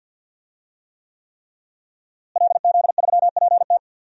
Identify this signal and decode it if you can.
Morse code, heard as a keyed tone, about 50 wpm, 700 Hz. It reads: LZ3PT
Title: DXCC-100 Call Signs - 5 Characters - Repeated 3x - 50wpm